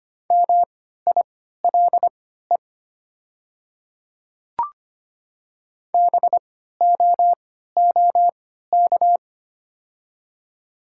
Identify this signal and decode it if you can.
Morse code, heard as a keyed tone, about 25 wpm, spaced as for 14 wpm, 700 Hz. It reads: MILE BOOK
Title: 3x Character Spacing - Top 500 Words - 25wpm